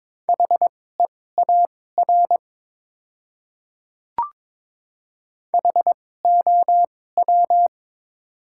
Morse code, heard as a keyed tone, about 22 wpm, spaced as for 16 wpm, 700 Hz. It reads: HEAR HOW
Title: 2x Character Spacing - Top 300 Words - 22wpm